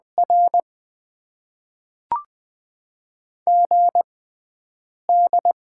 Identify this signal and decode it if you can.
Morse code, heard as a keyed tone, about 20 wpm, spaced as for 7 wpm, 700 Hz. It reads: R GD
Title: Top 90 QSO Elements - 6x Character Spacing - 20wpm